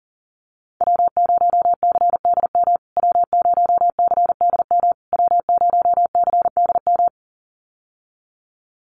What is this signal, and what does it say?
Morse code, heard as a keyed tone, about 40 wpm, 700 Hz. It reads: W0CDM W0CDM W0CDM
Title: USA Call Signs Repeated 3x 40wpm